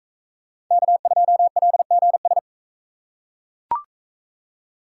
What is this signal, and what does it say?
Morse code, heard as a keyed tone, about 40 wpm, 700 Hz. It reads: K2LGS